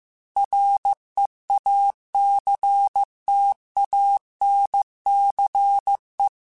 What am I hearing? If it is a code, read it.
Morse code, heard as a keyed tone, about 15 wpm, 800 Hz. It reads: REACTANCE